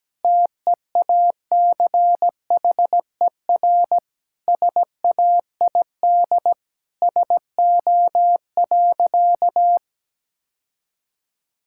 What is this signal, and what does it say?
Morse code, heard as a keyed tone, about 17 wpm, 700 Hz. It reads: TEACHER SAID SO.
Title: Sentences from Top 1000 Words - Review Part 6 - 17wpm